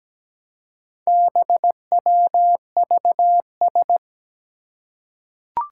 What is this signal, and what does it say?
Morse code, heard as a keyed tone, about 17 wpm, 700 Hz. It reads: BWVS
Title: Four Challenging Letter-Number Combinations 17wpm